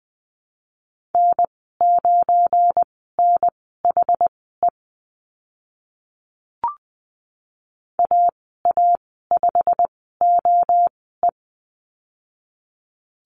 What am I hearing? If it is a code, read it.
Morse code, heard as a keyed tone, about 20 wpm, spaced as for 15 wpm, 700 Hz. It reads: N9NHE AA5OE
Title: 173 Lesson 28g - Number 9 - Call Signs 20wpm